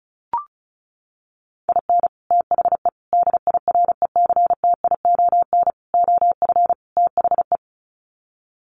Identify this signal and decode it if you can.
Morse code, heard as a keyed tone, about 34 wpm, 700 Hz. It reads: IN THE DIRECTION OF THE
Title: Sets of 5 Words 35wpm